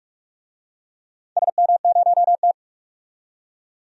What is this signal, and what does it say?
Morse code, heard as a keyed tone, about 46 wpm, 700 Hz. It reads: SM0T
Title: DXCC-100 Call Signs - 4 Characters - 45wpm